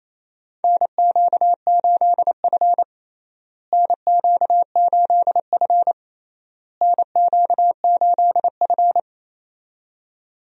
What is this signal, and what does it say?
Morse code, heard as a keyed tone, about 28 wpm, 700 Hz. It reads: NQ8F NQ8F NQ8F
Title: USA Call Signs - 4 Characters - Repeated 3x - 28wpm